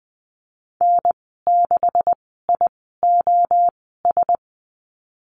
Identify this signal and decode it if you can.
Morse code, heard as a keyed tone, about 20 wpm, spaced as for 15 wpm, 700 Hz. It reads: N6IOS
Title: 154 Lesson 25g - Letter p - Call Signs 20wpm